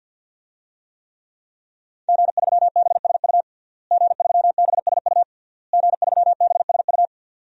Morse code, heard as a keyed tone, about 53 wpm, 700 Hz. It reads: G3BSU G3BSU G3BSU
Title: DXCC-100 Call Signs - 5 Characters - Repeated 3x - 50wpm